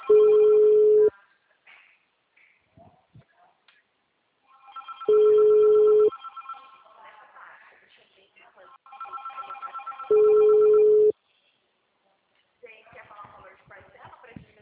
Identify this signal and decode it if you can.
Ringback tone (Europe, 425 Hz), with speech.